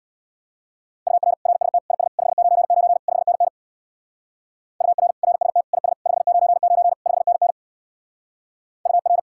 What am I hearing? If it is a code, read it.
Morse code, heard as a keyed tone, about 66 wpm, 700 Hz. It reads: UR RST IS 599 5NN UR RST IS 599 5NN UR